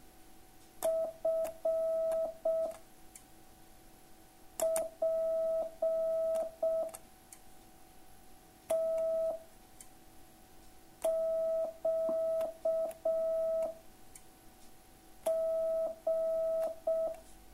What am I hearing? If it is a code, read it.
Morse code, heard as a keyed tone, about 6 wpm, 650 Hz. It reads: F P T Q G